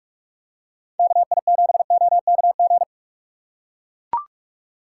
Morse code, heard as a keyed tone, about 44 wpm, 700 Hz. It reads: KI7OKG
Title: USA Call Signs - 2x Character Spacing - 45wpm